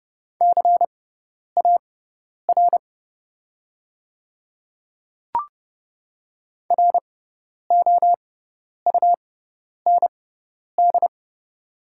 Morse code, heard as a keyed tone, about 30 wpm, spaced as for 10 wpm, 700 Hz. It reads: CAR ROUND
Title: 6x Character Spacing - Top 300 Words - 30wpm